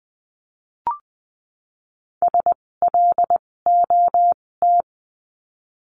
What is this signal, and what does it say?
Morse code, heard as a keyed tone, about 20 wpm, spaced as for 16 wpm, 700 Hz. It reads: SLOT
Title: 256 Lesson 42a - Words 12-20wpm